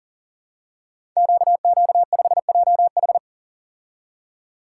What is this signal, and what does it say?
Morse code, heard as a keyed tone, about 42 wpm, 690 Hz. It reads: QQ5JH
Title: Code Groups - 5 Letter:Numbers - 40wpm